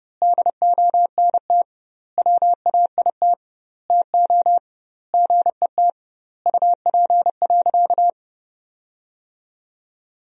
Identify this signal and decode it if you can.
Morse code, heard as a keyed tone, about 32 wpm, 700 Hz. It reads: DONT WAIT TO GET UP.
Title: 2x Word-Spacing - Sentences from Top 500 Words - 30wpm